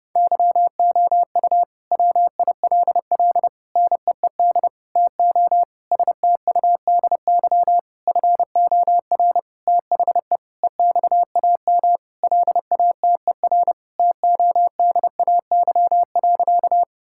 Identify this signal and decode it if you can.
Morse code, heard as a keyed tone, about 32 wpm, 700 Hz. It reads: YOU WILL NEED TO STUDY FOR THE EXAM LATER TODAY.
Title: Sentences from Top 1000 Words - Review Part 5 - 30wpm